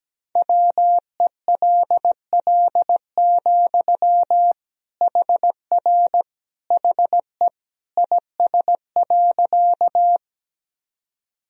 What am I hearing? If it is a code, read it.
Morse code, heard as a keyed tone, about 17 wpm, 700 Hz. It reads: WELL, HR HE IS.
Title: Sentences from Top 200 Words and Abbreviations 17wpm